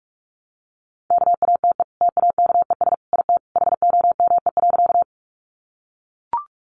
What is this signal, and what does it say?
Morse code, heard as a keyed tone, about 44 wpm, 700 Hz. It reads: KATE TAKES IT HOME.
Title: Easy Phrases 45wpm